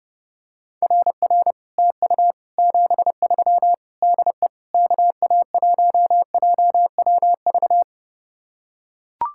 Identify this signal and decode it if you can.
Morse code, heard as a keyed tone, about 29 wpm, 700 Hz. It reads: RR TU 73 DE KA1JWV